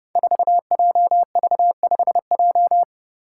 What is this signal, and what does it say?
Morse code, heard as a keyed tone, about 31 wpm, 700 Hz. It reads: 4JV5J
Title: Five Challenging Letter-Number Combinations 30wpm